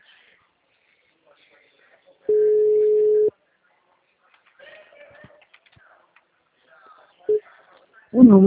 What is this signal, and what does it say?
Ringback tone (Europe, 425 Hz), with speech.
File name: out- -2008- - - .2039